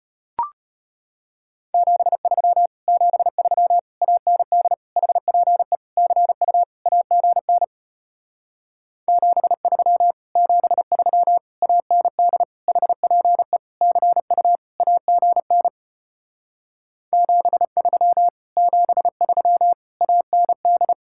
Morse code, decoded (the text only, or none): 73 73 AND HPE CU AGN 73 73 AND HPE CU AGN 73 73 AND